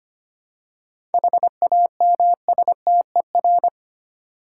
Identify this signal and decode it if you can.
Morse code, heard as a keyed tone, about 25 wpm, 700 Hz. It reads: HAMSTER
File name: practice / 25wpm / Easy Words with Hints - 25wpm